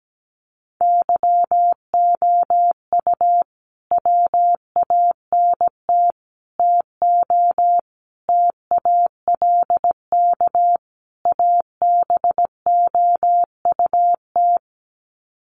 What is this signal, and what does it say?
Morse code, heard as a keyed tone, about 17 wpm, 700 Hz. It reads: YOU WANT TO TALK ABOUT
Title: Sets of 5 Words Encore 17wpm